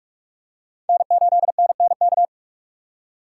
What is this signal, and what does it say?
Morse code, heard as a keyed tone, about 46 wpm, 700 Hz. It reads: N8NNK
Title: USA Call Signs - 5 Characters - 2x Character Spacing - 45wpm